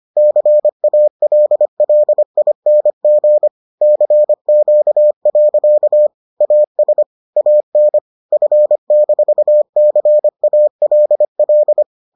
Morse code, heard as a keyed tone, about 25 wpm, 600 Hz. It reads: CALLING CQ. AS AN F-CALL